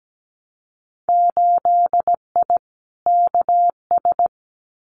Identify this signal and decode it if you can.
Morse code, heard as a keyed tone, about 17 wpm, 700 Hz. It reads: 8I KS
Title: Winter Field Day Exchanges Encore 17wpm